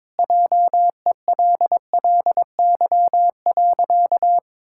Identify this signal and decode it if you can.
Morse code, heard as a keyed tone, about 22 wpm, 700 Hz. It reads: JELLY.
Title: Pangram Challenge v2 - 22wpm